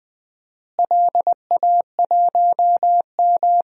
Morse code, heard as a keyed tone, about 20 wpm, 700 Hz. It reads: LA1M
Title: DXCC-100 Call Signs - 4 Characters - Repeated 3x - 20wpm